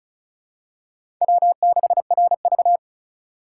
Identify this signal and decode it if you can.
Morse code, heard as a keyed tone, about 36 wpm, 700 Hz. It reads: W6RV
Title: USA Call Signs - 4 Characters - 4x Character Spacing - 35wpm